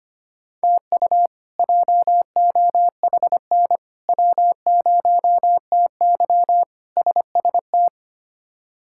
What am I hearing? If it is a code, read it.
Morse code, heard as a keyed tone, about 25 wpm, 700 Hz. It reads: TU JOHN W0TY SST
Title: K1USN SST QSO Practice - 25wpm